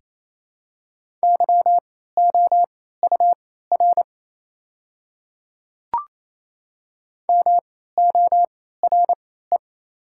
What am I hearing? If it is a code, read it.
Morse code, heard as a keyed tone, about 28 wpm, spaced as for 16 wpm, 700 Hz. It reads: YOUR MORE
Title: 3x Character Spacing - Top 200 Words - 28wpm